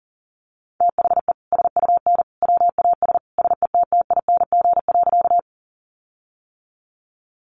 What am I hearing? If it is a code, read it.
Morse code, heard as a keyed tone, about 40 wpm, 700 Hz. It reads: THE SUN WAS SETTING.
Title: Sentences from Top 1300 Words 40wpm